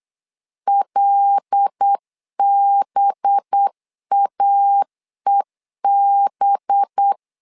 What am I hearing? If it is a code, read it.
Morse code, heard as a keyed tone, about 8 wpm, 800 Hz. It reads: LBAEB
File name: LEZIONI / 5 LBVC / 5 LB 8wpm 2816